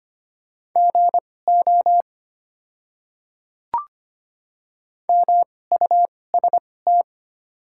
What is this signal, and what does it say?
Morse code, heard as a keyed tone, about 25 wpm, spaced as for 18 wpm, 700 Hz. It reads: GO MUST
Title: 2x Character Spacing - Top 300 Words - 25wpm